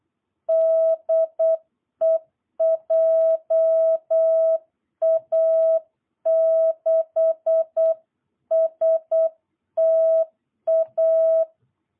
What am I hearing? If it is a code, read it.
Morse code, heard as a keyed tone, about 8 wpm, 660 Hz. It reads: DE JA6STA
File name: jh8cbh / 2022cw